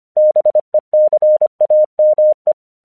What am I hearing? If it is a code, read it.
Morse code, heard as a keyed tone, about 25 wpm, 610 Hz. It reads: BECAME
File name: 1 English 25wpm / AQC13